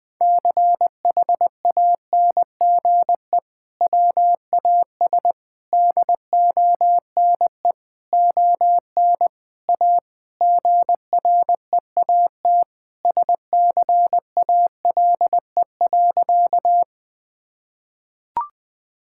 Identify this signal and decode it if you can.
Morse code, heard as a keyed tone, about 20 wpm, 700 Hz. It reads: CHANGE WAS DONE ON A GREAT SCALE.
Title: Sentences from Top 1000 Words - Review Part 5 - 20wpm